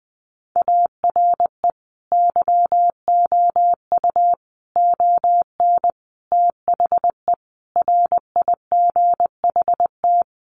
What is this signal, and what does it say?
Morse code, heard as a keyed tone, about 20 wpm, 700 Hz. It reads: ARE YOU ON THE RIGHT
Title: Sentences from Top 1100 Words 20wpm